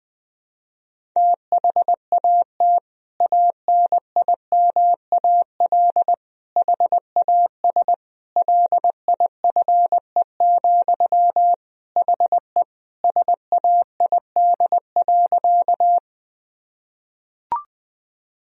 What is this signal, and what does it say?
Morse code, heard as a keyed tone, about 21 wpm, 700 Hz. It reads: THAT ANIMAL HAS LIFE, HE SAID.
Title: Sentences from Top 1000 Words - Review Part 1 - 20wpm